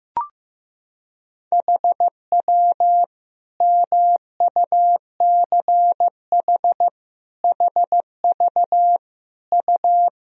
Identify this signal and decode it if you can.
Morse code, heard as a keyed tone, about 15 wpm, 700 Hz. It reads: HW MUCH HV U
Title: Sentences from Top 300 Words and Abbreviations 15wpm